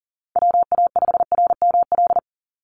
Morse code, heard as a keyed tone, about 40 wpm, 700 Hz. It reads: WA5RML
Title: Call Signs - 40wpm